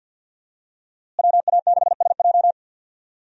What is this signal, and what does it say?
Morse code, heard as a keyed tone, about 53 wpm, 700 Hz. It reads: WA6SJ
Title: USA Call Signs - 5 Characters - Repeated 2x - 50wpm